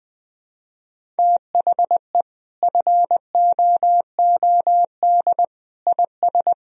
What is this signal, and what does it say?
Morse code, heard as a keyed tone, about 21 wpm, 700 Hz. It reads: THE FOOD IS
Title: Sentences from Top 1000 Words - Review Part 1 - 20wpm